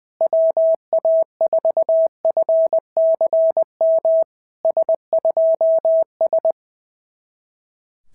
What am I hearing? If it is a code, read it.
Morse code, heard as a keyed tone, about 19 wpm, 650 Hz. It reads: WA4FCM S2S
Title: SOTA QSO Practice 20wpm